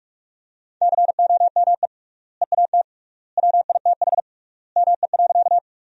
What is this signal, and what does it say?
Morse code, heard as a keyed tone, about 41 wpm, 700 Hz. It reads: COME EAT WITH ME.
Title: 3x Word-Spacing - Sentences from Top 300 Words - 45wpm